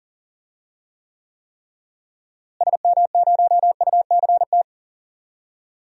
Morse code, heard as a keyed tone, about 40 wpm, 700 Hz. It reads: SM0UCT
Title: DXCC-100 Call Signs - 6 Characters - Repeated 3x - 40wpm